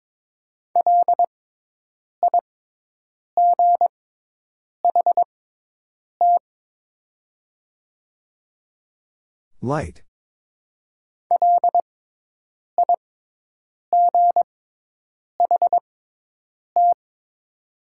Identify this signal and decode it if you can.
Morse code, heard as a keyed tone, about 22 wpm, spaced as for 8 wpm, 700 Hz. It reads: LIGHT LIGHT
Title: 6x Character Spacing - Top 500 Words - 22wpm